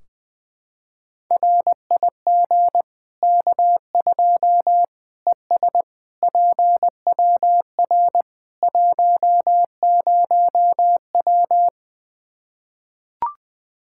Morse code, heard as a keyed tone, about 20 wpm, 700 Hz. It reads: RIG K2 ES PWR 10W
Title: Practice QSO - Exchange 2 - Info - 20wpm